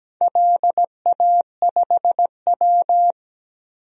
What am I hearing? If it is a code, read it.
Morse code, heard as a keyed tone, about 17 wpm, 700 Hz. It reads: LA5W
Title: DXCC-100 Call Signs - 4 Characters - 17wpm